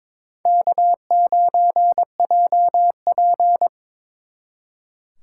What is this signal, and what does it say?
Morse code, heard as a keyed tone, about 26 wpm, 700 Hz. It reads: K9JP
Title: Speed Racing - USA Call Signs 22wpm